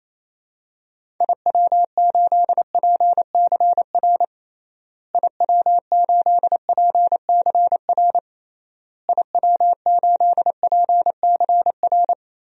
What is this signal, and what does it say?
Morse code, heard as a keyed tone, about 28 wpm, 700 Hz. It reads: IW8PCR IW8PCR IW8PCR
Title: DXCC-100 Call Signs - 6 Characters - Repeated 3x - 28wpm